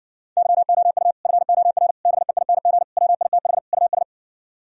Morse code, heard as a keyed tone, about 59 wpm, 700 Hz. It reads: YOU FOR BEING WITH US